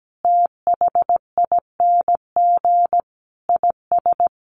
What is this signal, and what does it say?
Morse code, heard as a keyed tone, about 17 wpm, 700 Hz. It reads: THING IS
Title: Sentences from Top 1100 Words 17wpm